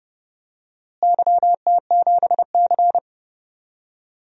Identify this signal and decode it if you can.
Morse code, heard as a keyed tone, about 30 wpm, 700 Hz. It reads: YT7C
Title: DXCC-100 Call Signs - 4 Characters - Repeated 3x - 30wpm